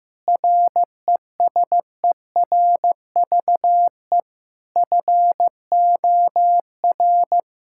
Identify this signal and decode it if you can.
Morse code, heard as a keyed tone, about 15 wpm, 700 Hz. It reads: RESERVE FOR
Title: Sentences from Top 1000 Words - Review Part 6 - 15wpm